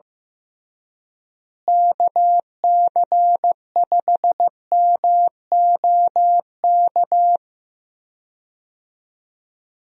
Morse code, heard as a keyed tone, about 15 wpm, 700 Hz. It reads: KC5MOK
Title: USA Call Signs - 6 Characters - Repeated 3x - 15wpm